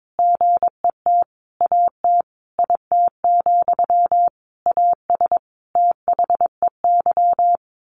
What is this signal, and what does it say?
Morse code, heard as a keyed tone, about 22 wpm, 700 Hz. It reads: GET AT IT, AS THEY
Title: A Sound Of Thunder 22wpm